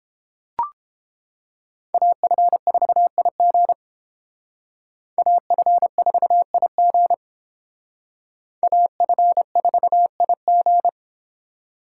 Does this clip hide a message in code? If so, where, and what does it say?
Yes: Morse code from 1.9 to 10.9 s, heard as a keyed tone, about 29 wpm, 700 Hz. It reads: AF4IG AF4IG AF4IG